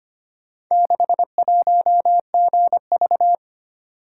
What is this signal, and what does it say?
Morse code, heard as a keyed tone, about 25 wpm, 700 Hz. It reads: 61GV
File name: practice / 25wpm / Four Letter-Number Combinations - 25wpm